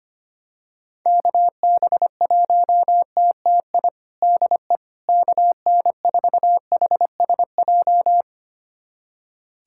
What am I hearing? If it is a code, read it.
Morse code, heard as a keyed tone, about 25 wpm, 700 Hz. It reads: KB1TTI DE KN4HSJ
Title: USA Callsign DE Callsign 25wpm